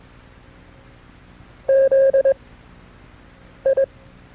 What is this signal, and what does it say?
Morse code, heard as a keyed tone, about 22 wpm, spaced as for 6 wpm, 540 Hz. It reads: ZI
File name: FreeFiles / CW-practice / KB6MT-tape2-side2